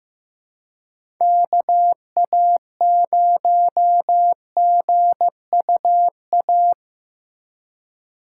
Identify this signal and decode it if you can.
Morse code, heard as a keyed tone, about 15 wpm, 700 Hz. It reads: KA0GUA
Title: USA Call Signs - 6 Characters - 4x Character Spacing - 15wpm